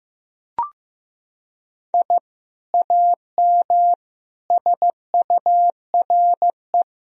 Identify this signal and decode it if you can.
Morse code, heard as a keyed tone, about 16 wpm, 700 Hz. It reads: I AM SURE
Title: Sentences from Top 1000 Words - Review Part 3 - 15wpm